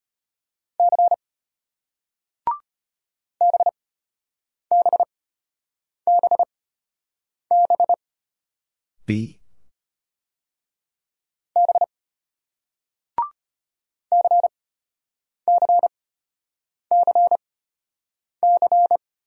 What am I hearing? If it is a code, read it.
Morse code, heard as a keyed tone, about 33 wpm, 700 Hz. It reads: C B B B B B C C C C